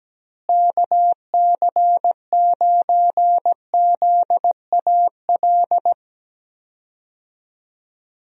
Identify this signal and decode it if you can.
Morse code, heard as a keyed tone, about 17 wpm, 700 Hz. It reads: KC9ZAL
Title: USA Call Signs - 6 Characters - Repeated 2x - 17wpm